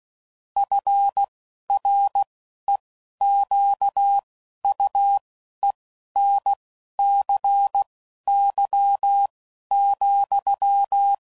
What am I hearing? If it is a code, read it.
Morse code, heard as a keyed tone, about 16 wpm, spaced as for 11 wpm, 800 Hz. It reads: FREQUENCY,
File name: CW Code Groups @10WPM / Technical Subjects / XL